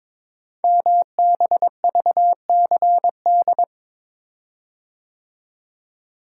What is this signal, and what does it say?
Morse code, heard as a keyed tone, about 22 wpm, 700 Hz. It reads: MBVCD
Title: Code Groups - 5 Letters - 22wpm